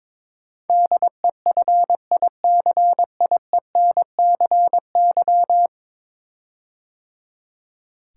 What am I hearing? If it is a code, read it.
Morse code, heard as a keyed tone, about 22 wpm, 700 Hz. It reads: DEFICIENCY
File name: practice / 22wpm / 10 Letter Words - First 3 Letters Given - 22wpm